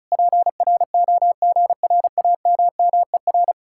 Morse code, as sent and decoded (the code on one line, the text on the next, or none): .--. .-. --- --. .-. .- -- -- . .-.
PROGRAMMER